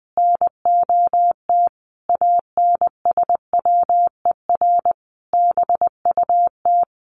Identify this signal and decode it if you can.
Morse code, heard as a keyed tone, about 19 wpm, 700 Hz. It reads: NOT ANSWER BUT
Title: Sentences from Top 600 Words 20wpm